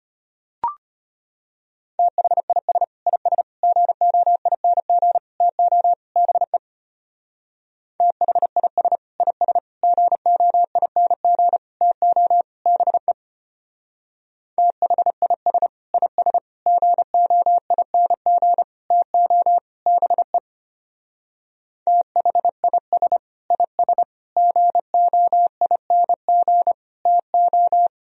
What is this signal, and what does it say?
Morse code, heard as a keyed tone, about 30 wpm, 700 Hz. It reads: THIS IS GOING TO BE THIS IS GOING TO BE THIS IS GOING TO BE THIS IS GOING TO
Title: Speed-Racing - Sets of 5 Words 25wpm